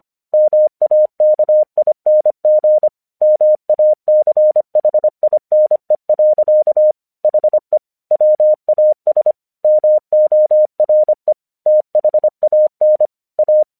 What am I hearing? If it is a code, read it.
Morse code, heard as a keyed tone, about 26 wpm, 610 Hz. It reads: MAKING MACHINE. HE WAS MORE THAN A
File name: 1 English 25wpm / AQC20